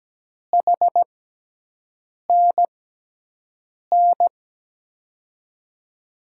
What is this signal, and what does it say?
Morse code, heard as a keyed tone, about 17 wpm, spaced as for 6 wpm, 700 Hz. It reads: HNN
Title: USA Call Signs - 6 Characters - 6x Character Spacing - 17wpm